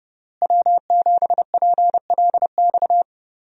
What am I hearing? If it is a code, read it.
Morse code, heard as a keyed tone, about 32 wpm, 700 Hz. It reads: W7PLX
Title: USA Call Signs - 5 Characters - Repeated 3x - 30wpm